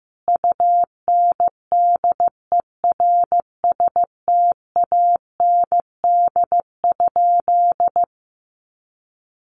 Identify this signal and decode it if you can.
Morse code, heard as a keyed tone, about 15 wpm, 700 Hz. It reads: UNDERSTAND?
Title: Sentences from Top 1100 Words 15wpm